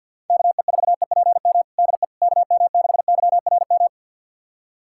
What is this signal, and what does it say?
Morse code, heard as a keyed tone, about 51 wpm, 700 Hz. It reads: KE4EPM DE KM6YRM
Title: USA Callsign DE Callsign - 6 Characters - 50wpm